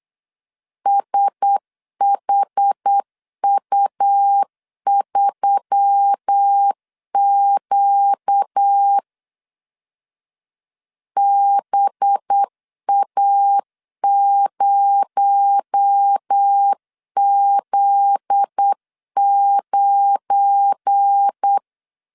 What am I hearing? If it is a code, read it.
Morse code, heard as a keyed tone, about 8 wpm, 800 Hz. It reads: SHU3Q BA0Z9